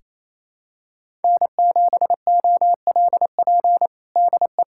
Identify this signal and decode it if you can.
Morse code, heard as a keyed tone, about 28 wpm, 700 Hz. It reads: N7OLP DE
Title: USA Callsign DE Callsign - 5 Characters - 28wpm